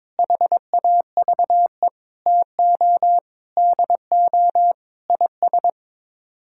Morse code, heard as a keyed tone, about 22 wpm, 700 Hz. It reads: HAVE TO DO IS
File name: practice / 22wpm / Sets of 6 Words - 22wpm